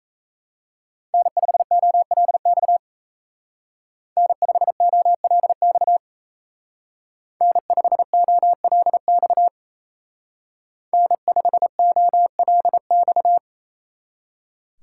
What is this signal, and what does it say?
Morse code, heard as a keyed tone, about 33 wpm, 700 Hz. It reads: N5OLX N5OLX N5OLX N5OLX
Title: Speed Racing - USA 1x3 Call Signs 28wpm